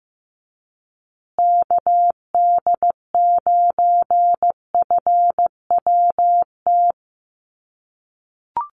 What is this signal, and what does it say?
Morse code, heard as a keyed tone, about 15 wpm, 700 Hz. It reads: KD9FWT
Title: USA Call Signs Repeated 2x 15wpm